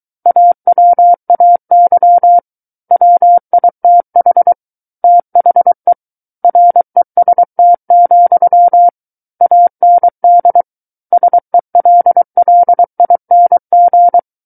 Morse code, heard as a keyed tone, about 23 wpm, 700 Hz. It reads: AWAY WITH THE REST, AND SELLING
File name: Book One / 14-Chapter 14 - 23 WPM